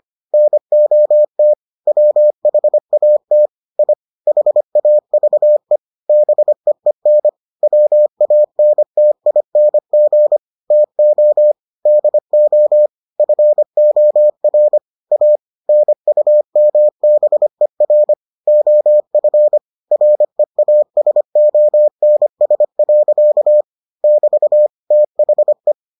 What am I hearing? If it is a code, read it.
Morse code, heard as a keyed tone, about 25 wpm, 600 Hz. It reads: NOT WHAT I HAVE BEEN WANTING TO DO FOR A NUMBER OF REASONS. <BT> THE